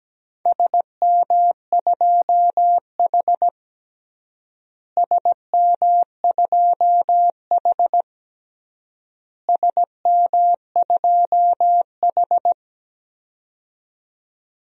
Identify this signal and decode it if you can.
Morse code, heard as a keyed tone, about 17 wpm, 700 Hz. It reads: SM2H SM2H SM2H
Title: DXCC-100 Call Signs - 4 Characters - Repeated 3x - 17wpm